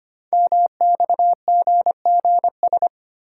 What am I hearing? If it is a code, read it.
Morse code, heard as a keyed tone, about 25 wpm, 700 Hz. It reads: MXGGS
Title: Code Groups - 5 Letters - 25wpm